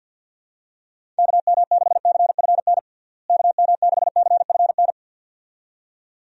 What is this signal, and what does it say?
Morse code, heard as a keyed tone, about 50 wpm, 700 Hz. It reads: KM6CFN KM6CFN
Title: USA Call Signs - 6 Characters - Repeated 2x - 50wpm